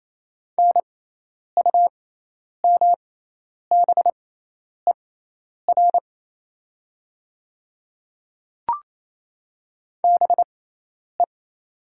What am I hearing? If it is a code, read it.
Morse code, heard as a keyed tone, about 28 wpm, spaced as for 10 wpm, 690 Hz. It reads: NUMBER BE
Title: 6x Character Spacing - Top 100 Words - 28wpm